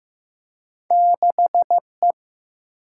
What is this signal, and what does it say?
Morse code, heard as a keyed tone, about 15 wpm, 700 Hz. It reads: 6E
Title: Single-Letter Single-Number 15wpm